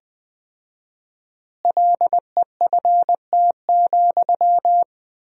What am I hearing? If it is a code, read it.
Morse code, heard as a keyed tone, about 20 wpm, 700 Hz. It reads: LEFT,
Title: 4x Word-Spacing - Sentences from Top 400 Words - 20wpm